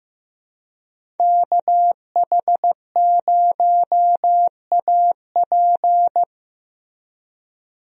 Morse code, heard as a keyed tone, about 16 wpm, 700 Hz. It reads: KH0AP
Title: USA Call Signs - 5 Characters - Repeated 3x - 15wpm